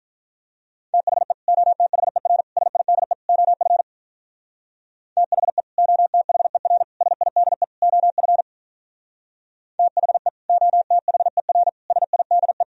Morse code, heard as a keyed tone, about 44 wpm, 700 Hz. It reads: THE OTHER SIDE OF THE OTHER SIDE OF THE OTHER SIDE